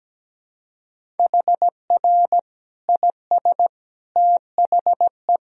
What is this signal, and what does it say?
Morse code, heard as a keyed tone, about 17 wpm, 700 Hz. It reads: HR IS THE
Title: Sentences from Top 300 Words and Abbreviations 17wpm